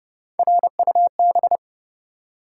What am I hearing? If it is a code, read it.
Morse code, heard as a keyed tone, about 30 wpm, 700 Hz. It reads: RUB
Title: Three Letter Words - 30wpm